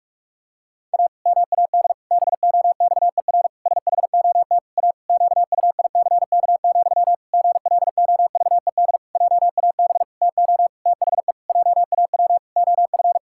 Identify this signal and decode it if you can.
Morse code, heard as a keyed tone, about 48 wpm, 700 Hz. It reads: A MAD BOXER SHOT A QUICK, GLOVED JAB TO THE JAW OF
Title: Pangram Challenge v2 - 45wpm